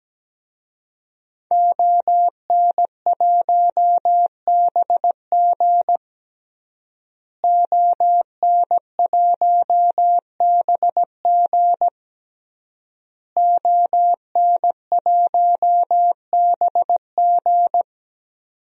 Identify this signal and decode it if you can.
Morse code, heard as a keyed tone, about 17 wpm, 700 Hz. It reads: ON1BG ON1BG ON1BG
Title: DXCC-100 Call Signs - 5 Characters - Repeated 3x - 17wpm